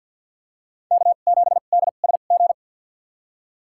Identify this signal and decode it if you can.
Morse code, heard as a keyed tone, about 48 wpm, spaced as for 35 wpm, 690 Hz. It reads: K7DSG
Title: USA Call Signs - 2x Character Spacing - 50wpm